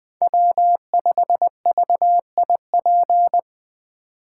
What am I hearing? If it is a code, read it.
Morse code, heard as a keyed tone, about 21 wpm, 700 Hz. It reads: W5VIP